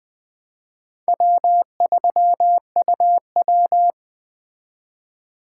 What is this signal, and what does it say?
Morse code, heard as a keyed tone, about 19 wpm, 700 Hz. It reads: W3UW